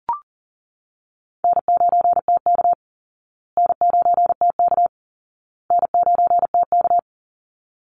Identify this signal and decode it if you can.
Morse code, heard as a keyed tone, about 40 wpm, 700 Hz. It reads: N9TK N9TK N9TK
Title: Easy Call Signs Repeated 3x - 40wpm